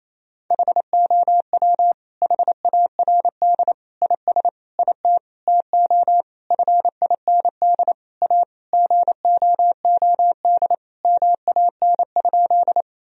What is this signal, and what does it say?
Morse code, heard as a keyed tone, about 28 wpm, 700 Hz. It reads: HOW HARD IS IT TO FIND A GOOD MAN?
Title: Sentences from Top 1000 Words - Review Part 1 - 28wpm